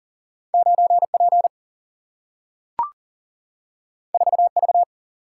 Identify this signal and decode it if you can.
Morse code, heard as a keyed tone, about 40 wpm, 700 Hz. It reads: 9 P 4 V
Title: Single-Letter Single-Number 40wpm